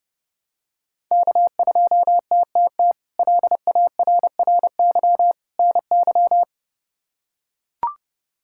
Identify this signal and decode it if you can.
Morse code, heard as a keyed tone, about 30 wpm, 700 Hz. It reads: K2TTT LARRY NY
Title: K1USN SST QSO Practice - 30wpm